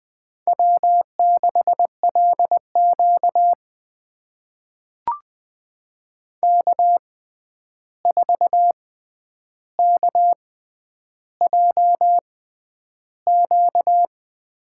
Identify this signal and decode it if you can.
Morse code, heard as a keyed tone, about 20 wpm, spaced as for 10 wpm, 700 Hz. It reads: W6LQ K4KJQ
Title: USA Call Signs - 6x Character Spacing - 20wpm